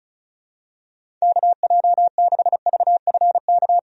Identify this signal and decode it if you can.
Morse code, heard as a keyed tone, about 36 wpm, 700 Hz. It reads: KJ6VFK
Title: USA Call Signs - 6 Characters - 35wpm